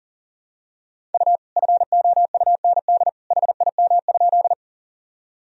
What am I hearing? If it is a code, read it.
Morse code, heard as a keyed tone, about 40 wpm, 700 Hz. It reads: U FOUND HIM?
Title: Sentences from Top 300 Words and Abbreviations 40wpm